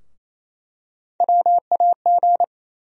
Morse code, heard as a keyed tone, about 28 wpm, 700 Hz. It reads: WAG
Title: USA Call Sign Postfixes - 3 Letters 28wpm